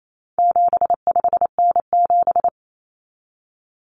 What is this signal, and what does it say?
Morse code, heard as a keyed tone, about 28 wpm, 700 Hz. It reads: 75N7
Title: Common Cut Numbers - 4-Digit 28wpm